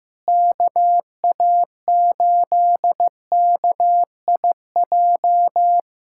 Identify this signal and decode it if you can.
Morse code, heard as a keyed tone, about 16 wpm, 700 Hz. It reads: KA8KIJ
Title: USA Callsign DE Callsign - 6 Characters - 15wpm